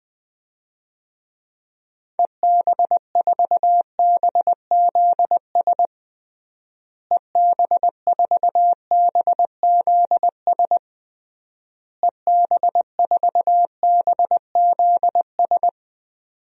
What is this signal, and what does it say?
Morse code, heard as a keyed tone, about 20 wpm, 700 Hz. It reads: EB4BZS EB4BZS EB4BZS
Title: DXCC-100 Call Signs - 6 Characters - Repeated 3x - 20wpm